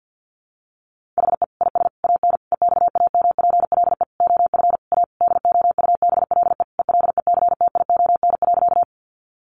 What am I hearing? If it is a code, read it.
Morse code, heard as a keyed tone, about 50 wpm, 700 Hz. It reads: HE IS AN EXAMPLE OF A DOUBLE ELECTION.
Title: Sentences from Top 1500 Words 50wpm